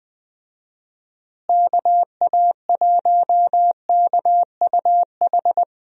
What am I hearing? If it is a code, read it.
Morse code, heard as a keyed tone, about 21 wpm, 700 Hz. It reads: KA1KUH
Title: USA Call Signs - 6 Characters - Repeated 3x - 20wpm